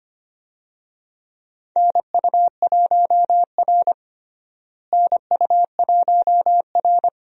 Morse code, heard as a keyed tone, about 25 wpm, 700 Hz. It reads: NU1R NU1R
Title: USA Call Signs - 4 Characters - Repeated 3x - 25wpm